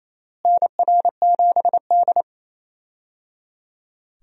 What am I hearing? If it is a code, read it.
Morse code, heard as a keyed tone, about 28 wpm, 700 Hz. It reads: NR7D
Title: USA Call Signs - 4 Characters - Repeated 3x - 28wpm